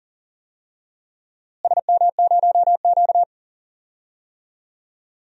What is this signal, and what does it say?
Morse code, heard as a keyed tone, about 40 wpm, 700 Hz. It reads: SM0Q
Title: DXCC-100 Call Signs - 4 Characters - Repeated 2x - 40wpm